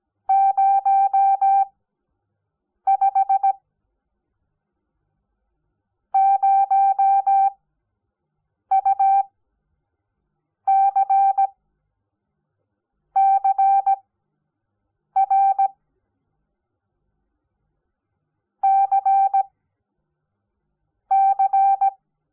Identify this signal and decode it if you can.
Morse code, heard as a keyed tone, about 17 wpm, 780 Hz. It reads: 0 5 0 U C C R C C
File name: Onlinelehrgang / morsen / DJ4UF Morsen05